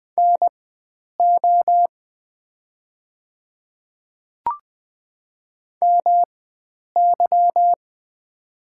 Morse code, heard as a keyed tone, about 20 wpm, spaced as for 9 wpm, 700 Hz. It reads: NO MY
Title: 4x Character Spacing - Top 300 Words - 20wpm